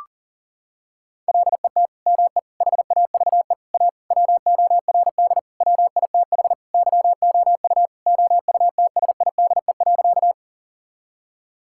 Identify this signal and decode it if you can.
Morse code, heard as a keyed tone, about 40 wpm, 700 Hz. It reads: LET ME HAVE A WORD WITH YOU OUTSIDE.
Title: Sentences from Top 1000 Words - Review Part 6 - 40wpm